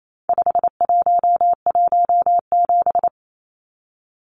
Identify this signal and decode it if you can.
Morse code, heard as a keyed tone, about 28 wpm, 700 Hz. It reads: 5117
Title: Numbers - 4-Digit 28wpm